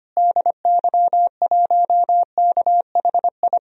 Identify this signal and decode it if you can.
Morse code, heard as a keyed tone, about 25 wpm, 700 Hz. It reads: DY1KHI
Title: DXCC-100 Call Signs - 6 Characters - Repeated 2x - 25wpm